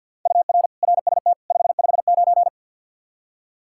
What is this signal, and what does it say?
Morse code, heard as a keyed tone, about 52 wpm, 700 Hz. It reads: UR RST 559
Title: K1USN SST QSO Practice 50wpm